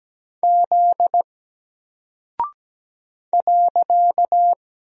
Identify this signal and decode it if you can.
Morse code, heard as a keyed tone, about 17 wpm, 700 Hz. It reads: Z .
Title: Single Character (Letter:Number:Punctuation:Prosign) 17wpm